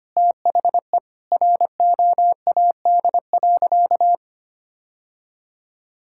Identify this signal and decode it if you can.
Morse code, heard as a keyed tone, about 25 wpm, 700 Hz. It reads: THE ROAD.
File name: practice / 25wpm / Sentences with 4-Word Phrases - 25wpm